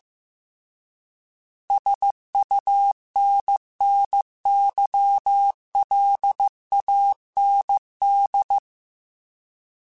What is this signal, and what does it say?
Morse code, heard as a keyed tone, about 15 wpm, 800 Hz. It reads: SUNNYLAND